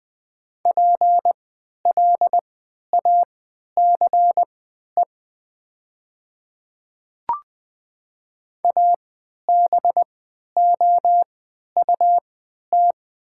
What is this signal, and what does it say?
Morse code, heard as a keyed tone, about 20 wpm, spaced as for 11 wpm, 690 Hz. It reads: PLACE ABOUT
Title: 3x Character Spacing - Top 200 Words - 20wpm